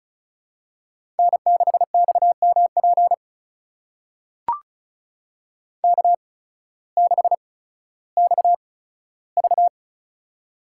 Morse code, heard as a keyed tone, about 36 wpm, spaced as for 16 wpm, 700 Hz. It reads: N6XMP K6XV
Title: USA Call Signs - 8x Character Spacing - 35wpm